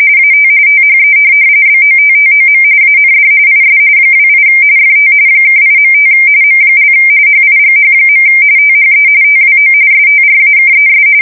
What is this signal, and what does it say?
RTTY, about 45.45 baud. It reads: AT THE BOTTOM AND IS 28 FT (8.5 M) HIGH. THE TWO STRAKES
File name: fsk170 / USS-Missouri